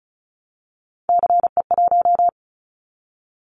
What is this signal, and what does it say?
Morse code, heard as a keyed tone, about 36 wpm, 700 Hz. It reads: CE1
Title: Callsign Prefixes for Top 100 DXCC - 35wpm